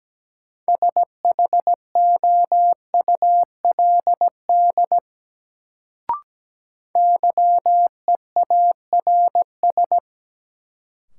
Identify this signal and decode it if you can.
Morse code, heard as a keyed tone, about 17 wpm, 700 Hz. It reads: SHOULD YEARS
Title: Sets of 2 Words - New Word Drill - 17wpm